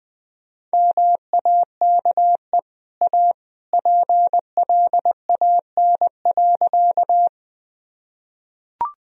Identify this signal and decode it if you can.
Morse code, heard as a keyed tone, about 20 wpm, 700 Hz. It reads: MAKE A PLAN.
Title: Sentences from Top 1000 Words - Review Part 2 - 20wpm